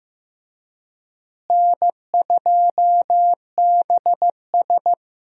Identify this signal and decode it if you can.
Morse code, heard as a keyed tone, about 15 wpm, 700 Hz. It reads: N2BS
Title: USA Call Signs - 4 Characters - Repeated 3x - 15wpm